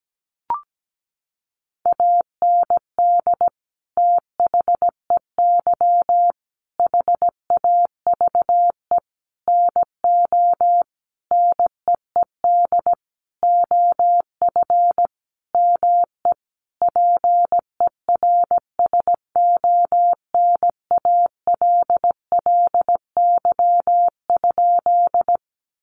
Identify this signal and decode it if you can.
Morse code, heard as a keyed tone, about 17 wpm, 700 Hz. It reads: AND THEY HAVE NO NEED OF ME PERSONALLY?
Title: Sentences from Top 1400 Words 17wpm